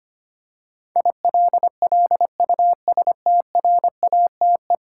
Morse code, heard as a keyed tone, about 25 wpm, 700 Hz. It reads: ILLUSTRATE
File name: Ten Letter Words - 25wpm